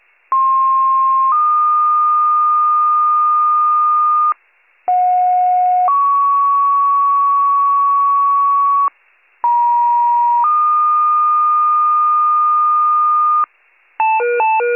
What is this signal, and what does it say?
Two-tone page, heard as a one-second tone, then a three-second tone, about 1060 then 1220 Hz, about 730 then 1120 Hz, about 950 then 1220 Hz. It is heard from 0.3 to 13.4 s.